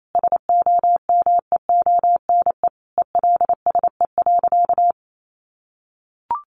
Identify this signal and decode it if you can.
Morse code, heard as a keyed tone, about 28 wpm, 700 Hz. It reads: SOMEONE ELSE.
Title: Sentences from Top 800 Words 28wpm